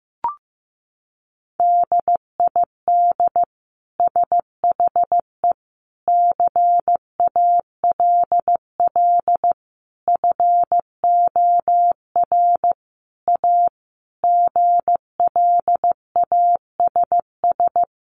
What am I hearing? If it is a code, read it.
Morse code, heard as a keyed tone, about 15 wpm, 700 Hz. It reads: DID SHE CALL FOR A GLASS
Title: Sentences from Top 1300 Words 15wpm